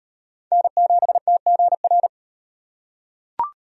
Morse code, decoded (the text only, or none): N7TGR